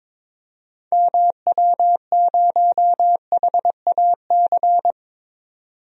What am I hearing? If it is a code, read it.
Morse code, heard as a keyed tone, about 22 wpm, 700 Hz. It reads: MW0HAC
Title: DXCC-100 Call Signs - 6 Characters - 22wpm